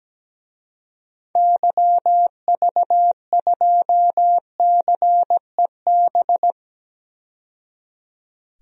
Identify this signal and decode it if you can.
Morse code, heard as a keyed tone, about 17 wpm, 700 Hz. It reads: YV2CEB
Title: DXCC-100 Call Signs - 6 Characters - 17wpm